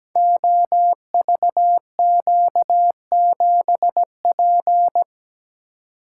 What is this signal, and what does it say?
Morse code, heard as a keyed tone, about 17 wpm, 700 Hz. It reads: OVQ7P
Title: Code Groups - 5 Letter:Numbers - 17wpm